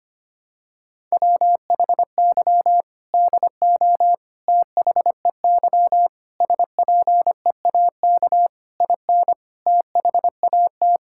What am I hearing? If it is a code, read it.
Morse code, heard as a keyed tone, about 25 wpm, 700 Hz. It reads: WHY DO THEY SPEAK IN THAT
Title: Sentences from Top 1600 Words 25wpm